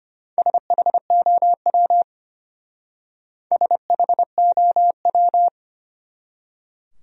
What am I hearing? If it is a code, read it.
Morse code, heard as a keyed tone, about 32 wpm, 700 Hz. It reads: SHOW SHOW